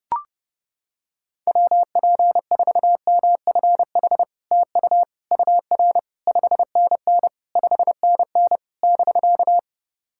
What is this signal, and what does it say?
Morse code, heard as a keyed tone, about 30 wpm, 700 Hz. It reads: WP4MFH TU UR 5NN 5NN <BK>
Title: SOTA QSO Practice 30wpm